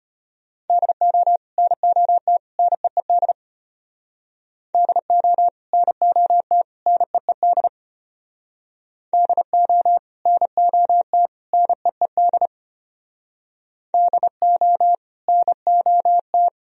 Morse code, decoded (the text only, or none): DO NOT NEED DO NOT NEED DO NOT NEED DO NOT